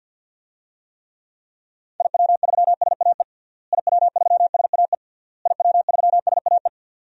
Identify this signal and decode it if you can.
Morse code, heard as a keyed tone, about 52 wpm, 700 Hz. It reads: IW3SAE IW3SAE IW3SAE
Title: DXCC-100 Call Signs - 6 Characters - Repeated 3x - 50wpm